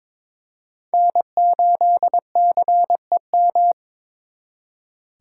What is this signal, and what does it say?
Morse code, heard as a keyed tone, about 26 wpm, 700 Hz. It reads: N8CEM
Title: Speed Racing - USA 1x3 Call Signs 22wpm